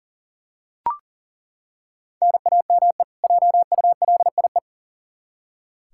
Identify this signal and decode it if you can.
Morse code, heard as a keyed tone, about 40 wpm, 700 Hz. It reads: NAME JULIE
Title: K1USN SST QSO Practice 40wpm